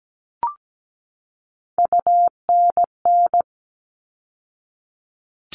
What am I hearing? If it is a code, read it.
Morse code, heard as a keyed tone, about 17 wpm, 700 Hz. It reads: UNN
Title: All Cut Numbers - 3-Digit 17wpm